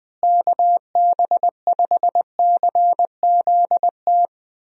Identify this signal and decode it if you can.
Morse code, heard as a keyed tone, about 20 wpm, 700 Hz. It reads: KB5CZT
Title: USA Call Signs - 6 Characters - Repeated 3x - 20wpm